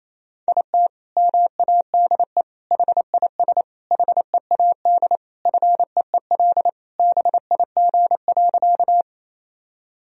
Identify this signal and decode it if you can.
Morse code, heard as a keyed tone, about 28 wpm, 700 Hz. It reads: IT MADE HIS HEAD FEEL BIG.
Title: Sentences from Top 1000 Words - Review Part 1 - 28wpm